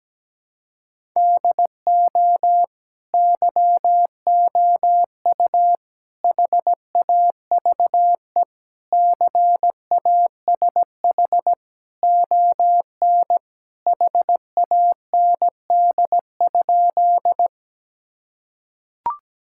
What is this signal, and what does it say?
Morse code, heard as a keyed tone, about 17 wpm, 700 Hz. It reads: DO YOU HAVE CASH ON HAND?
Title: Sentences from Top 1000 Words - Review Part 5 - 17wpm